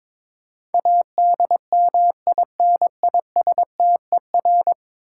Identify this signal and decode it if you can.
Morse code, heard as a keyed tone, about 22 wpm, 700 Hz. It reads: ADMINISTER
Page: morse-code-ninja-practice-22wpm